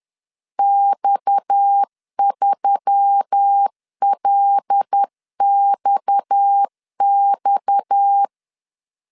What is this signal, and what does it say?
Morse code, heard as a keyed tone, about 10 wpm, 800 Hz. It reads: X3LXX
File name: LEZIONI / 6 PKX0 / 6 PKX 10wpm 2426